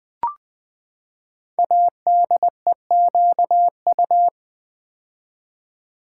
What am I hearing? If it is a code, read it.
Morse code, heard as a keyed tone, about 20 wpm, 700 Hz. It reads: ADEQU
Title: Build A Word - 5 Letters At A Time - 10-Letter Words - 20wpm